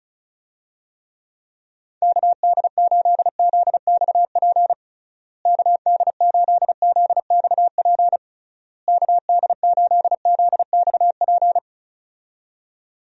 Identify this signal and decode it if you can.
Morse code, heard as a keyed tone, about 36 wpm, 700 Hz. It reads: KD8ZXP KD8ZXP KD8ZXP
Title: USA Call Signs - 6 Characters - Repeated 3x - 35wpm